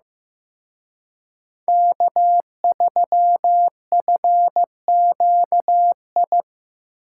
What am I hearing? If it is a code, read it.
Morse code, heard as a keyed tone, about 18 wpm, 700 Hz. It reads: K3FQI
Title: Speed Racing - USA 1x3 Call Signs 15wpm